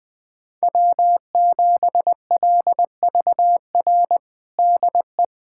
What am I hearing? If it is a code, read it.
Morse code, heard as a keyed tone, about 20 wpm, 700 Hz. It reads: W7LVR DE
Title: USA Callsign DE Callsign - 5 Characters - 20wpm